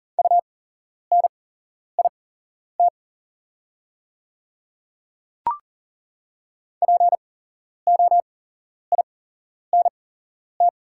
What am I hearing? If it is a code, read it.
Morse code, heard as a keyed tone, about 40 wpm, spaced as for 11 wpm, 690 Hz. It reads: UNIT POINT
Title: 8x Character Spacing - Top 500 Words - 40wpm